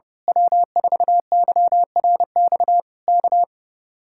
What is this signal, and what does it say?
Morse code, heard as a keyed tone, about 24 wpm, 700 Hz. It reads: W4YRX K
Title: Speed-Racing Common QSO Phrases - 25wpm